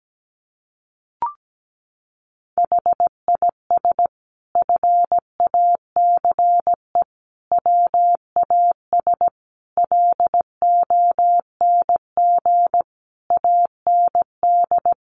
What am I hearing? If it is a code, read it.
Morse code, heard as a keyed tone, about 17 wpm, 700 Hz. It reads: HIS FACE WAS LONG AND
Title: Sentences from Top 1200 Words 17wpm